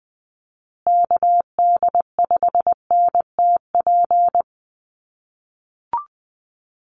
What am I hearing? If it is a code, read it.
Morse code, heard as a keyed tone, about 19 wpm, 700 Hz. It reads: KD5NTP
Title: USA Call Signs Repeated 2x 20wpm